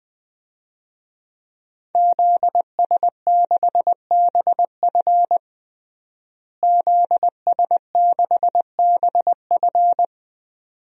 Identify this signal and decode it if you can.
Morse code, heard as a keyed tone, about 20 wpm, 700 Hz. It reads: ZS6BF ZS6BF
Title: DXCC-100 Call Signs - 5 Characters - Repeated 2x - 20wpm